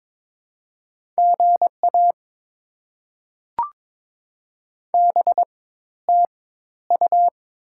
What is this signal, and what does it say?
Morse code, heard as a keyed tone, about 22 wpm, spaced as for 13 wpm, 690 Hz. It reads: GA BTU
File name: Top 20 QSO Elements - 4x Character Spacing - 22wpm